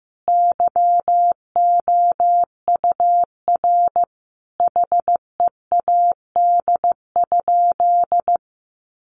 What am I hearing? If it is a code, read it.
Morse code, heard as a keyed tone, about 15 wpm, 700 Hz. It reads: YOUR HEAD?
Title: Sentences from Top 600 Words 15wpm